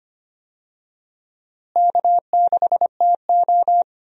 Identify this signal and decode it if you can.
Morse code, heard as a keyed tone, about 25 wpm, 700 Hz. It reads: K6TO
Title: USA Call Signs - 4 Characters - Repeated 2x - 25wpm